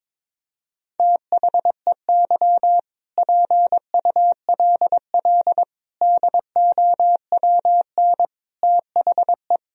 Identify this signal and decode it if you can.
Morse code, heard as a keyed tone, about 21 wpm, 700 Hz. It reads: THEY PULL DOWN THE
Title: Sentences from Top 1000 Words - Review Part 2 - 22wpm